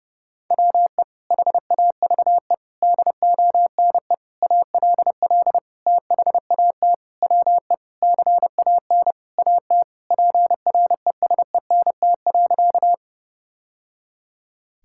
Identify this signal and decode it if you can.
Morse code, heard as a keyed tone, about 28 wpm, 700 Hz. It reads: WE HAVE DONE ALL THAT WE CAN AT PRESENT.
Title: Sentences from Top 1000 Words - Review Part 5 - 30wpm